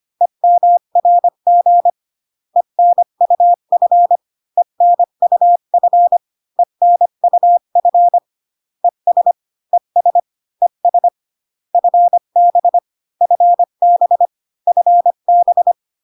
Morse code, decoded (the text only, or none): EMRG ENUF ENUF ENUF ES ES ES FB FB FB